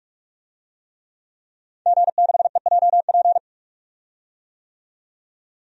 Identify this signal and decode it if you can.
Morse code, heard as a keyed tone, about 45 wpm, 700 Hz. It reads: G6EJP
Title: DXCC-100 Call Signs - 5 Characters - 45wpm